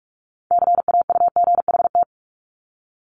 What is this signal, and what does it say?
Morse code, heard as a keyed tone, about 46 wpm, 700 Hz. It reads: CAUGHT
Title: Multi-letter Phonemes 45wpm